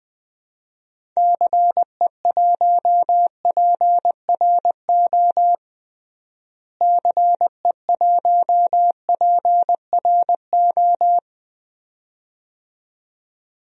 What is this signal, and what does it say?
Morse code, heard as a keyed tone, about 20 wpm, 700 Hz. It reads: CE1PRO CE1PRO
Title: DXCC-100 Call Signs - 6 Characters - Repeated 2x - 20wpm